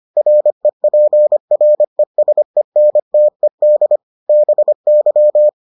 Morse code, heard as a keyed tone, about 25 wpm, 600 Hz. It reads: REPRESENTED BY